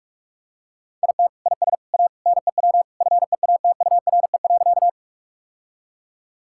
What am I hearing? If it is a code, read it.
Morse code, heard as a keyed tone, about 46 wpm, 700 Hz. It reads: IT IS A NEW FEATURE.
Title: Sentences from Top 1000 Words - Review Part 5 - 45wpm